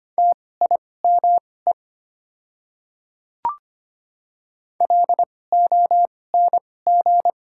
Morse code, heard as a keyed tone, about 25 wpm, spaced as for 18 wpm, 690 Hz. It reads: TIME LONG
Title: 2x Character Spacing - Top 200 Words - 25wpm